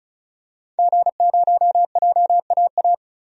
Morse code, heard as a keyed tone, about 36 wpm, 700 Hz. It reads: G0JAA
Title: DXCC-100 Call Signs - 5 Characters - 35wpm